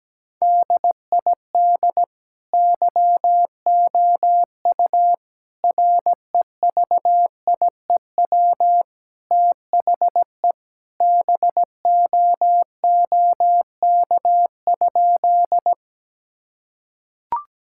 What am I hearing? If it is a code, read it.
Morse code, heard as a keyed tone, about 17 wpm, 700 Hz. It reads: DID YOU REVIEW THE BOOK?
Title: Sentences from Top 1000 Words - Review Part 6 - 17wpm